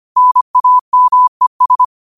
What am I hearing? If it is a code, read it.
Morse code, heard as a keyed tone, about 25 wpm, 1000 Hz. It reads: NAMES
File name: qotdfiles / 25wpm